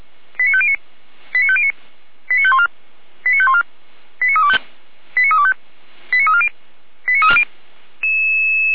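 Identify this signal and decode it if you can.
Five-tone selective call (ZVEI), heard as quick runs of five tones, tones of about 70 ms. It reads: 89509 89509 89514 89514 89325 89325 89340 89340
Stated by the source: ZVEI five-tone call received by SDR, from feuerwehr-orlamuende.org.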